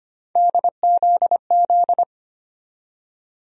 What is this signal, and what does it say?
Morse code, heard as a keyed tone, about 25 wpm, 700 Hz. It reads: DZZ
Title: USA Call Sign Postfixes - 3 Letters 25wpm